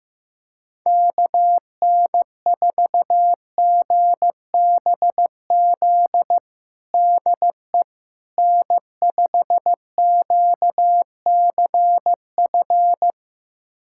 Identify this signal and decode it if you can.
Morse code, heard as a keyed tone, about 15 wpm, 700 Hz. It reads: KN4GBZ DE N5QCF
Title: USA Callsign DE Callsign 15wpm